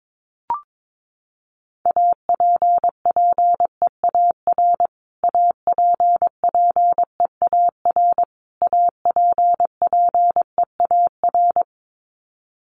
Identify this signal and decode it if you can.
Morse code, heard as a keyed tone, about 22 wpm, 700 Hz. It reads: APPEAR APPEAR APPEAR
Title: Top 500 Words Repeated 22wpm